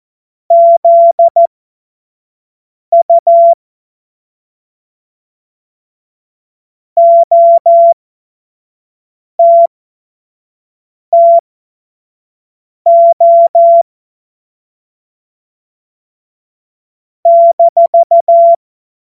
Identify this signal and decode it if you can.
Morse code, heard as a keyed tone, about 14 wpm, spaced as for 5 wpm, 680 Hz. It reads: ZU OTTO -
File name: Morsekurs / Witze / Witz 12 März 2024 25 F70